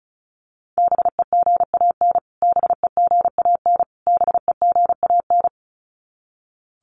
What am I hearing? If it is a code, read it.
Morse code, heard as a keyed tone, about 34 wpm, 700 Hz. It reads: BEGAN BEGAN BEGAN
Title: Top 500 Words Repeated 35wpm